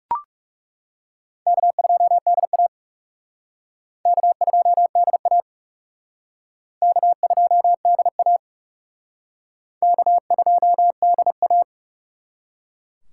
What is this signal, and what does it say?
Morse code, heard as a keyed tone, about 37 wpm, 700 Hz. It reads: K2DA K2DA K2DA K2DA
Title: Speed Racing - USA 1x2 Call Signs 30wpm